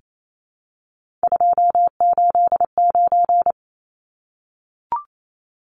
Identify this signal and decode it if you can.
Morse code, heard as a keyed tone, about 28 wpm, 700 Hz. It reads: 289
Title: Numbers - 3-Digit 28wpm